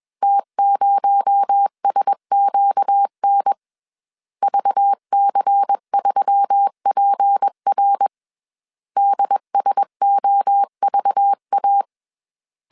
Morse code, heard as a keyed tone, about 21 wpm, 800 Hz. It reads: T0SQN VC3PR DSOVA
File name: CW / DET / 1 6 1500C S A 20wpm 2048